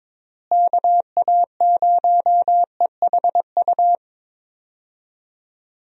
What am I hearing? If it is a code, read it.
Morse code, heard as a keyed tone, about 22 wpm, 700 Hz. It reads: KA0EHU
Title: USA Call Signs - 6 Characters - Repeated 3x - 22wpm